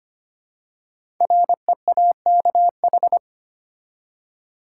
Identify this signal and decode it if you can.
Morse code, heard as a keyed tone, about 25 wpm, 700 Hz. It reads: REAKH
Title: Code Groups - 5 Letter:Numbers - 25wpm